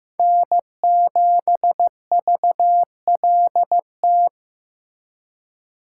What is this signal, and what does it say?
Morse code, heard as a keyed tone, about 18 wpm, 700 Hz. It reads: N7VLT
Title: Speed Racing - USA 1x3 Call Signs 15wpm